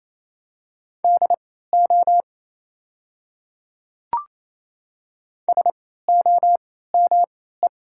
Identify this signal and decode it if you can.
Morse code, heard as a keyed tone, about 28 wpm, spaced as for 16 wpm, 700 Hz. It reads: DO SOME
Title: 3x Character Spacing - Top 200 Words - 28wpm